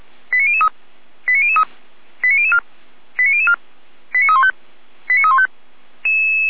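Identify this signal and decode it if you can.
Five-tone selective call (ZVEI), heard as quick runs of five tones, tones of about 70 ms. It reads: 89003 89003 89004 89004 89315 89315
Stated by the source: ZVEI five-tone selective call 89001 received with an SDR.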